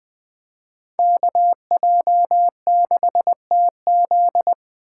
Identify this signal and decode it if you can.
Morse code, heard as a keyed tone, about 21 wpm, 700 Hz. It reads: KJ6TZ